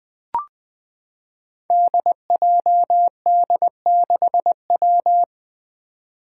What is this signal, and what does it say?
Morse code, heard as a keyed tone, about 20 wpm, 700 Hz. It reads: DJD6W
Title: Five Challenging Letter-Number Combinations 20wpm